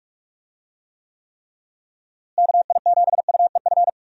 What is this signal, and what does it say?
Morse code, heard as a keyed tone, about 44 wpm, 700 Hz. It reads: KI7UEF
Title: USA Call Signs - 6 Characters - Repeated 3x - 45wpm